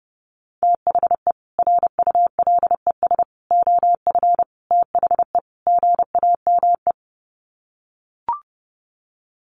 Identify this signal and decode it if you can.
Morse code, heard as a keyed tone, about 30 wpm, 700 Hz. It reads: THE RULES OF THE GAME
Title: Sets of 5 Words Encore2 30wpm